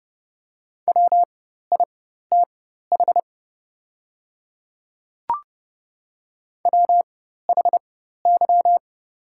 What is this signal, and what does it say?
Morse code, heard as a keyed tone, about 30 wpm, spaced as for 14 wpm, 700 Hz. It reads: WITH WHY
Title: 4x Character Spacing - Top 200 Words - 30wpm